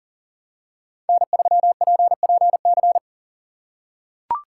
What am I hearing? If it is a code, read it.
Morse code, heard as a keyed tone, about 40 wpm, 690 Hz. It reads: N3PPC
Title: USA Call Signs - 4x Character Spacing - 40wpm